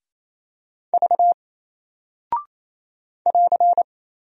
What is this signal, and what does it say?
Morse code, heard as a keyed tone, about 27 wpm, 700 Hz. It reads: V <AR>
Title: Single Character (Letter:Number:Punctuation:Prosign) 28wpm